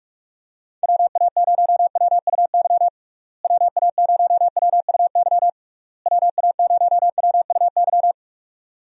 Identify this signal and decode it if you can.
Morse code, heard as a keyed tone, about 46 wpm, 700 Hz. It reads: WA0WUY WA0WUY WA0WUY
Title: USA Call Signs - 6 Characters - Repeated 3x - 45wpm